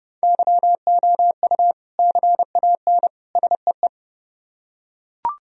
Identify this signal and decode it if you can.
Morse code, heard as a keyed tone, about 28 wpm, 700 Hz. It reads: YOU CAN SEE